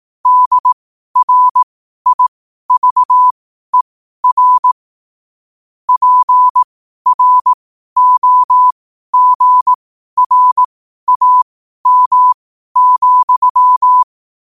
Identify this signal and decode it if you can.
Morse code, heard as a keyed tone, about 18 wpm, spaced as for 12 wpm, 1000 Hz. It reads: DRIVER PROGRAM,
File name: qotdfiles / 12wpm